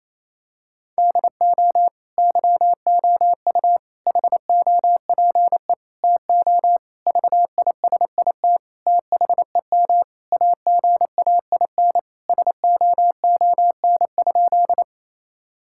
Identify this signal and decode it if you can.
Morse code, heard as a keyed tone, about 28 wpm, 700 Hz. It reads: DO YOU HOPE TO VISIT THEM AGAIN SOON?
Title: Sentences from Top 1000 Words - Review Part 6 - 28wpm